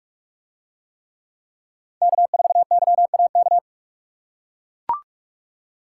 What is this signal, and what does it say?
Morse code, heard as a keyed tone, about 41 wpm, 700 Hz. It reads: K4YAK